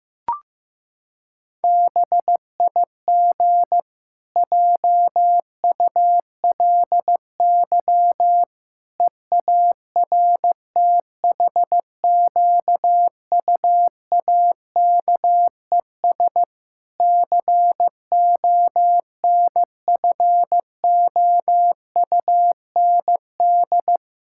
Morse code, heard as a keyed tone, about 15 wpm, 700 Hz. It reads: BIG JULY EARTHQUAKES CONFOUND
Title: Pangram Challenge v2 - 15wpm